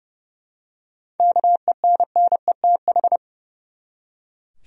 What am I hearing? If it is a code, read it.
Morse code, heard as a keyed tone, about 32 wpm, 700 Hz. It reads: KENNETH
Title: USA Common Names - 30wpm